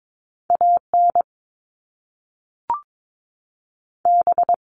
Morse code, heard as a keyed tone, about 22 wpm, 700 Hz. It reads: A N B
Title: Top Two Letter Combinations in Words 22wpm